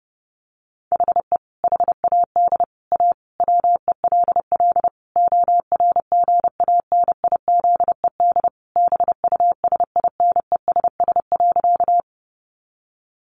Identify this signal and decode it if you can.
Morse code, heard as a keyed tone, about 30 wpm, 700 Hz. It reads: HE HAD A WELL ORGANIZED BUSINESS.
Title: Sentences from Top 1300 Words 30wpm